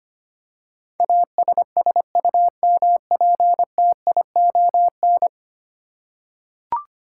Morse code, heard as a keyed tone, about 25 wpm, 700 Hz. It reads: ASSUMPTION
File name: Ten Letter Words - 25wpm